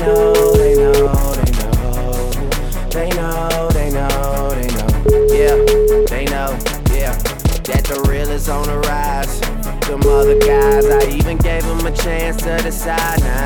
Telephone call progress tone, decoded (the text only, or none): ringback tone